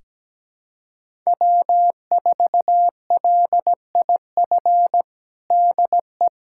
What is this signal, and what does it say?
Morse code, heard as a keyed tone, about 17 wpm, 700 Hz. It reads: W4LIF DE
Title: USA Callsign DE Callsign - 5 Characters - 17wpm